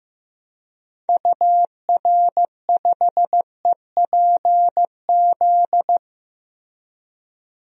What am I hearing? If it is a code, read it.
Morse code, heard as a keyed tone, about 16 wpm, 700 Hz. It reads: UR5EPZ
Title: DXCC-100 Call Signs - 6 Characters - Repeated 2x - 15wpm